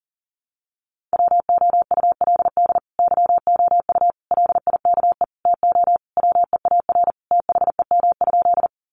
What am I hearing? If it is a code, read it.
Morse code, heard as a keyed tone, about 40 wpm, 700 Hz. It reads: WOULD YOU LIKE TO WEAR THEM?
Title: Sentences from Top 1500 Words 40wpm